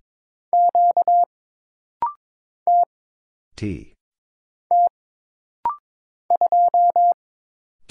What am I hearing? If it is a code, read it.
Morse code, heard as a keyed tone, about 22 wpm, 700 Hz. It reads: Q T T 2
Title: Single Letter-Number - Rapid-Fire 22wpm